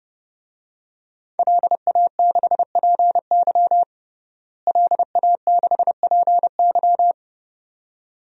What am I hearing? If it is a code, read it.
Morse code, heard as a keyed tone, about 30 wpm, 700 Hz. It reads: LA6PY LA6PY
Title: DXCC-100 Call Signs - 5 Characters - Repeated 2x - 30wpm